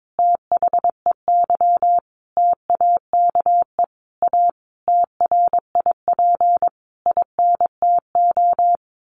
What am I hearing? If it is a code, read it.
Morse code, heard as a keyed tone, about 22 wpm, 700 Hz. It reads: THEY TAKE A TRIP INTO
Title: Sentences from Top 1300 Words 22wpm